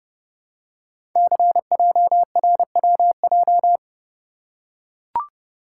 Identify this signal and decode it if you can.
Morse code, heard as a keyed tone, about 30 wpm, 690 Hz. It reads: CJRWJ
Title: Code Groups - 5 Letters - 30wpm